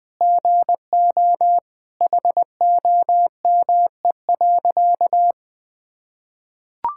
Morse code, heard as a keyed tone, about 19 wpm, 700 Hz. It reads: GO HOME.
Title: Sentences from Top 1000 Words - Review Part 6 - 20wpm